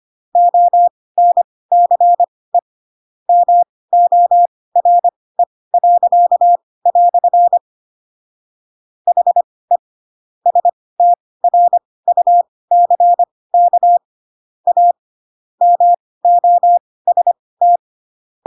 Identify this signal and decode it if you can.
Morse code, heard as a keyed tone, about 25 wpm, spaced as for 18 wpm, 700 Hz. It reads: ONCE MORE." HE STRUCK A MOST
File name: Pinocchio / Pinocchio ch1 12 25 ik1hge